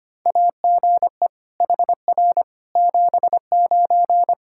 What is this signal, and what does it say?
Morse code, heard as a keyed tone, about 25 wpm, 700 Hz. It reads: AGE HR 79
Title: Practice QSO - Exchange 3 - More Info - 25wpm